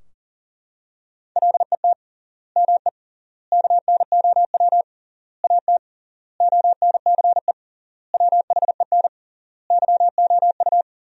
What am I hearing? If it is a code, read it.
Morse code, heard as a keyed tone, about 40 wpm, 700 Hz. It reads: LET ME KNOW AT ONCE WHEN YOU
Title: 3x Word-Spacing - Sentences from Top 300 Words - 40wpm